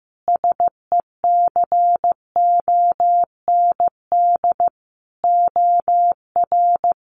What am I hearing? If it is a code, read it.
Morse code, heard as a keyed tone, about 15 wpm, 700 Hz. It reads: SECOND OR
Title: Sentences from Top 1500 Words 15wpm